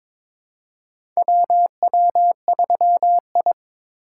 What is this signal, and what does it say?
Morse code, heard as a keyed tone, about 22 wpm, 700 Hz. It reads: WW3I
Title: USA Call Signs - 4 Characters - 4x Character Spacing - 22wpm